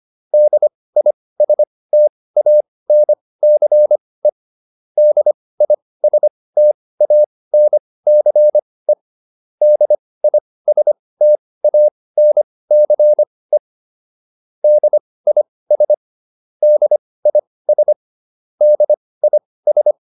Morse code, heard as a keyed tone, about 25 wpm, spaced as for 18 wpm, 600 Hz. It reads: DISTANCE DISTANCE DISTANCE DIS DIS DIS
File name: DIS8 18